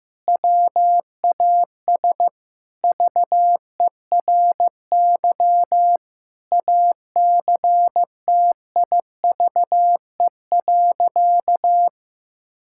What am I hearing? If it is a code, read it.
Morse code, heard as a keyed tone, about 15 wpm, 700 Hz. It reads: WAS VERY ACTIVE.
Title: Sentences from Top 1000 Words - Review Part 6 - 15wpm